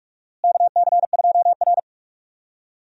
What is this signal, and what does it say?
Morse code, heard as a keyed tone, about 42 wpm, 700 Hz. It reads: KC2R
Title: USA Call Signs - 4 Characters - Repeated 2x - 45wpm